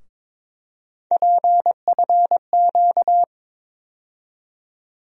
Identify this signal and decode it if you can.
Morse code, heard as a keyed tone, about 22 wpm, 700 Hz. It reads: PFQ
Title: USA Call Sign Postfixes - 3 Letters 22wpm